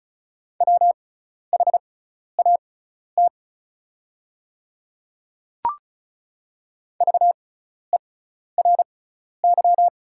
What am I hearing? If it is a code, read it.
Morse code, heard as a keyed tone, about 34 wpm, spaced as for 12 wpm, 700 Hz. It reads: WHAT VERY
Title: 6x Character Spacing - Top 200 Words - 35wpm